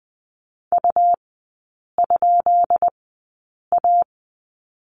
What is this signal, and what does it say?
Morse code, heard as a keyed tone, about 20 wpm, spaced as for 8 wpm, 700 Hz. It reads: U?A
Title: 172 Lesson 28e - Number 9 - Three Characters 20wpm